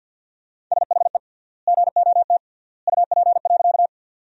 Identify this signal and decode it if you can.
Morse code, heard as a keyed tone, about 52 wpm, 700 Hz. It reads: SHE GOT UP.
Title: 3x Word-Spacing - Sentences from Top 300 Words - 50wpm